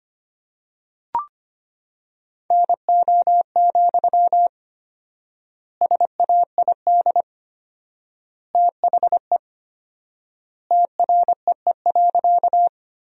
Morse code, heard as a keyed tone, about 25 wpm, 700 Hz. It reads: NO, SAID THE TREE.
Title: 4x Word-Spacing - Sentences from Top 300 Words - 25wpm